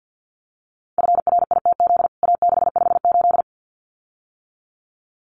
Morse code, heard as a keyed tone, about 50 wpm, 700 Hz. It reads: FRITZ A658
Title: CWT Exchange 50wpm